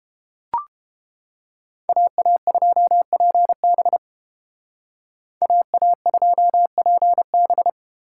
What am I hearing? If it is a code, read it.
Morse code, heard as a keyed tone, about 31 wpm, 700 Hz. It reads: AA2PB AA2PB